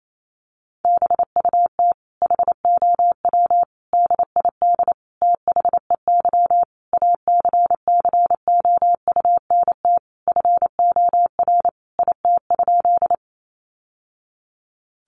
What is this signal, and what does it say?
Morse code, heard as a keyed tone, about 28 wpm, 700 Hz. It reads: BUT HOW DID THEY ACCOUNT FOR IT?
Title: Sentences from Top 900 Words 28wpm